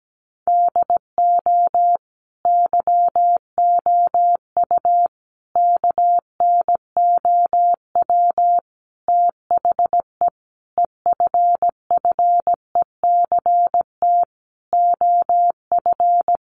Morse code, heard as a keyed tone, about 17 wpm, 700 Hz. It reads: DO YOU KNOW THE EFFECT OF
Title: Sentences from Top 1400 Words 17wpm